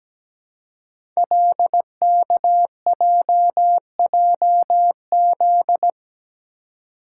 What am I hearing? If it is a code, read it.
Morse code, heard as a keyed tone, about 17 wpm, 700 Hz. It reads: LKJJZ
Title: Code Groups - 5 Letters - 17wpm